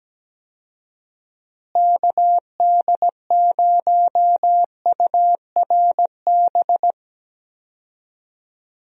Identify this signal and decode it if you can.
Morse code, heard as a keyed tone, about 17 wpm, 700 Hz. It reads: KD0URB
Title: USA Call Signs - 6 Characters - Repeated 3x - 17wpm